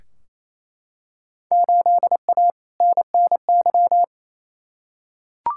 Morse code, heard as a keyed tone, about 28 wpm, 700 Hz. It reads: 8A NNY
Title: Field Day Exchanges Encore - 28wpm